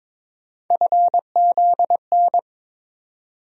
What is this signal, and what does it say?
Morse code, heard as a keyed tone, about 22 wpm, 700 Hz. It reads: FZN
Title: USA Call Sign Postfixes - 3 Letters 22wpm